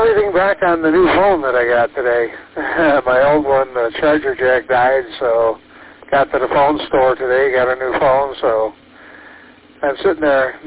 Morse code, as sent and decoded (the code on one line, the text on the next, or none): none